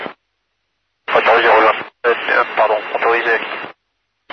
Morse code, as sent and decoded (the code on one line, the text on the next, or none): none